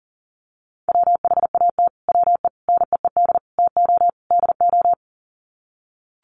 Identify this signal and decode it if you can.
Morse code, heard as a keyed tone, about 40 wpm, 700 Hz. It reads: WHAT WE NEED TO DO
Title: Sets of 5 Words Encore 40wpm